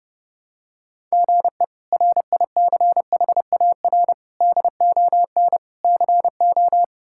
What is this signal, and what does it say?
Morse code, heard as a keyed tone, about 30 wpm, 700 Hz. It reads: GE RICHAR DON CO
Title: K1USN SST QSO Practice - 30wpm